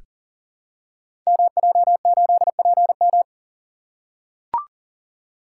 Morse code, heard as a keyed tone, about 41 wpm, 700 Hz. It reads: MJ8PM
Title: Code Groups - 5 Letter:Numbers - 40wpm